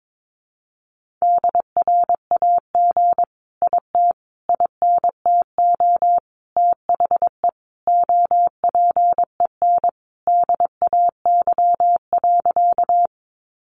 Morse code, heard as a keyed tone, about 22 wpm, 700 Hz. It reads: DRAG IT INTO THE OPEN DAY.
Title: Sentences from Top 1500 Words 22wpm